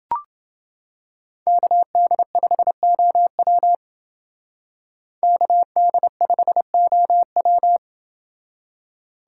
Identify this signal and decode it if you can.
Morse code, heard as a keyed tone, about 29 wpm, 700 Hz. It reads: KD5OW KD5OW